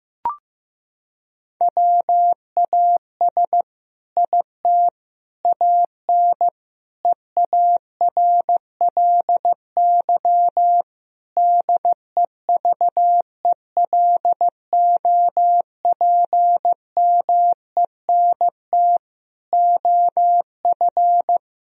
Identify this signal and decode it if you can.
Morse code, heard as a keyed tone, about 15 wpm, 700 Hz. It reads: WAS IT AN EARLY DEVELOPMENT OF
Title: Sentences from Top 1000 Words - Review Part 4 - 15wpm